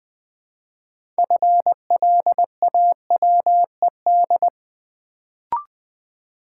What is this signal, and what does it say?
Morse code, heard as a keyed tone, about 20 wpm, 700 Hz. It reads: FLAWED
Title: Build A Word - 3 Letters At A Time - 6-Letter Words - 20wpm